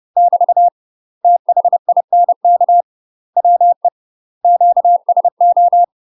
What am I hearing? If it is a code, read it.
Morse code, heard as a keyed tone, about 30 wpm, 700 Hz. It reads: <BT> THINK WE QSO
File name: UK MASTER QSOs TRAINING / 30WPM / UK MASTER QSOs 0047